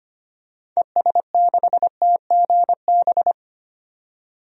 Morse code, heard as a keyed tone, about 25 wpm, 700 Hz. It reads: ES6TGB
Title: DXCC-100 Call Signs - 6 Characters - 25wpm